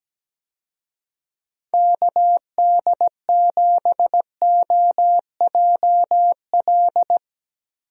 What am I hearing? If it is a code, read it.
Morse code, heard as a keyed tone, about 17 wpm, 700 Hz. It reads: KD7OJL
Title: USA Call Signs - 6 Characters - Repeated 2x - 17wpm